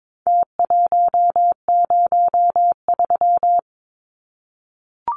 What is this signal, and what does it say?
Morse code, heard as a keyed tone, about 22 wpm, 700 Hz. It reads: T103
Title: Numbers - Contest Style - 4-Digit 22wpm